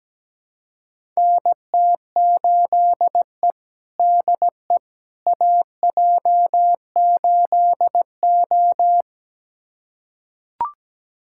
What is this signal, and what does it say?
Morse code, heard as a keyed tone, about 17 wpm, 700 Hz. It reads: NT8E DE AJ8O
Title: USA Callsign DE Callsign - 4 Characters - 17wpm